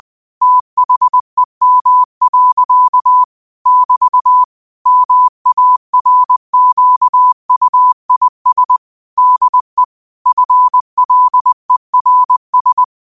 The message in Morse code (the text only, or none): THEM. <BT> MARQUIS DE FLERS